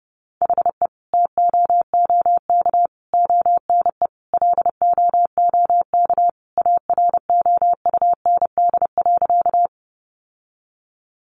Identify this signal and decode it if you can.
Morse code, heard as a keyed tone, about 30 wpm, 700 Hz. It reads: HE TOOK ONE LOOK AROUND.
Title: Sentences from Top 600 Words 30wpm